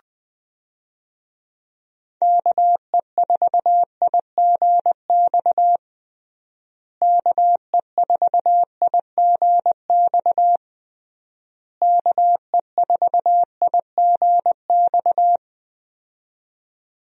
Morse code, heard as a keyed tone, about 21 wpm, 700 Hz. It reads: KE4IGX KE4IGX KE4IGX
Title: USA Call Signs - 6 Characters - Repeated 3x - 20wpm